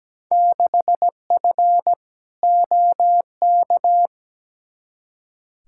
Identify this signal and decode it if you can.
Morse code, heard as a keyed tone, about 17 wpm, 700 Hz. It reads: 6F OK
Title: Field Day Exchanges Encore - 17wpm